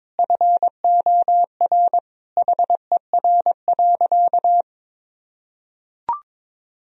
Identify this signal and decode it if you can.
Morse code, heard as a keyed tone, about 22 wpm, 700 Hz. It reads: FOR HER.
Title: Sentences from Top 1000 Words - Review Part 2 - 22wpm